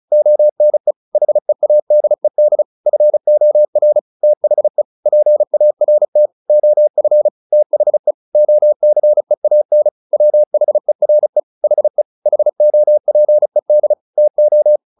Morse code, heard as a keyed tone, about 34 wpm, 600 Hz. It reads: ONE HEADED FOR THE PART OF THE OCEAN WHERE HE HOPED TO
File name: the old man and the sea / the old man and the sea 1 35wpm